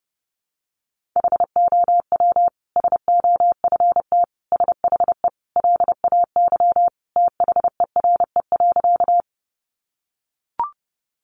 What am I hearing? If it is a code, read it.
Morse code, heard as a keyed tone, about 30 wpm, 700 Hz. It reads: HOW SOFT SHE LAY THERE.
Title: Sentences from Top 1500 Words 30wpm